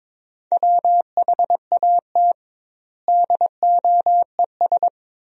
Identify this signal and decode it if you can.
Morse code, heard as a keyed tone, about 22 wpm, 700 Hz. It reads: WHAT DOES
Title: 2x Word-Spacing - Sentences from Top 400 Words - 22wpm